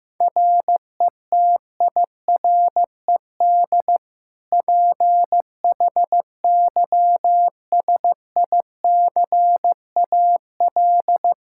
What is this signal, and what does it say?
Morse code, heard as a keyed tone, about 15 wpm, 700 Hz. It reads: RETIRED PHYSICAL
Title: Practice QSO - Exchange 3 - More Info - 15wpm